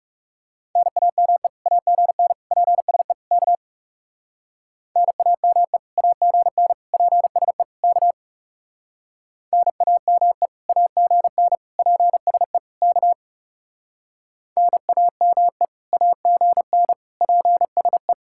Morse code, heard as a keyed tone, about 36 wpm, 700 Hz. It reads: NAME AGN PSE K NAME AGN PSE K NAME AGN PSE K NAME AGN PSE
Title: Speed-Racing Common QSO Phrases Encore - 30wpm